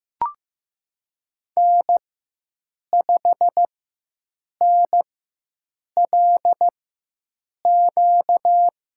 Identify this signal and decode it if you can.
Morse code, heard as a keyed tone, about 15 wpm, spaced as for 7 wpm, 700 Hz. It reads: N5NLQ
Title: USA Call Signs - 5 Characters - 4x Character Spacing - 15wpm